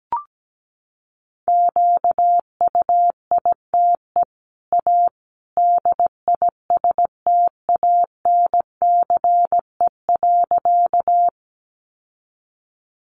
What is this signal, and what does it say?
Morse code, heard as a keyed tone, about 17 wpm, 700 Hz. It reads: QUITE A DISTANCE.
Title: Sentences from Top 1200 Words 17wpm